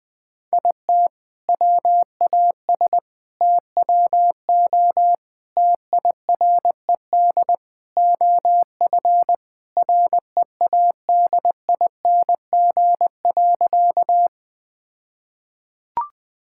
Morse code, heard as a keyed tone, about 20 wpm, 700 Hz. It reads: IT WAS TWO TIRED OF READING.
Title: All-Age Appropriate Jokes - 20wpm